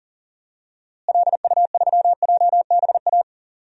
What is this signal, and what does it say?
Morse code, heard as a keyed tone, about 40 wpm, 700 Hz. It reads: LU3JBA
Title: DXCC-100 Call Signs - 6 Characters - Repeated 2x - 40wpm